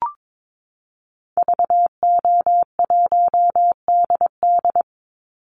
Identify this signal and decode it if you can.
Morse code, heard as a keyed tone, about 22 wpm, 700 Hz. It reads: VO1DD
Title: Call Signs Repeated 2x - 22wpm